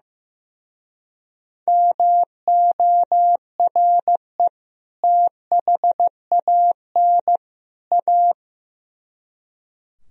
Morse code, heard as a keyed tone, about 17 wpm, 700 Hz. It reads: MORE THAN A
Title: Speed-Racing - Sets of 3 Words 15wpm